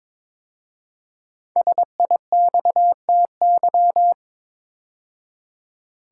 Morse code, heard as a keyed tone, about 22 wpm, 700 Hz. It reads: SIXTY
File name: 5 Letter Words - First 3 Letters Given - 22wpm